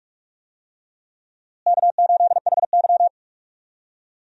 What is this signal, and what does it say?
Morse code, heard as a keyed tone, about 46 wpm, 700 Hz. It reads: K8HY
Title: USA Call Signs - 4 Characters - Repeated 2x - 45wpm